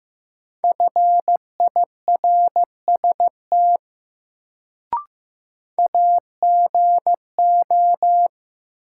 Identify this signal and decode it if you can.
Morse code, heard as a keyed tone, about 15 wpm, 700 Hz. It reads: FIRST AGO
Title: Sets of 2 Words - New Word Drill - 15wpm